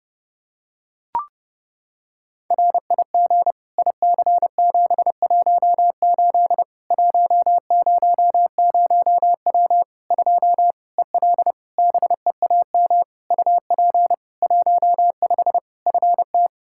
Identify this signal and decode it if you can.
Morse code, heard as a keyed tone, about 30 wpm, 700 Hz. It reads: RIG IC718 100W 2 EL BEAM UP 15 FT
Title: Practice QSO - Exchange 2 - Info - 30wpm